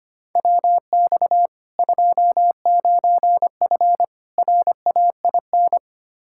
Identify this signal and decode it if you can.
Morse code, heard as a keyed tone, about 25 wpm, 700 Hz. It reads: WX 29F RAIN
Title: K1USN SST QSO Practice 25wpm